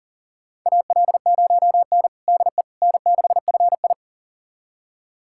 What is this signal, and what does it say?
Morse code, heard as a keyed tone, about 40 wpm, 700 Hz. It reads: AL0N DE N6FI
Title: USA Callsign DE Callsign - 4 Characters - 40wpm